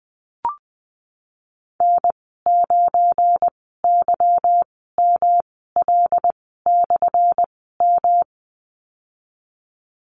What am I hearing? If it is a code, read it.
Morse code, heard as a keyed tone, about 20 wpm, spaced as for 15 wpm, 700 Hz. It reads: N9YML/M
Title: 179 Lesson 29g - Slash - Call Signs 20wpm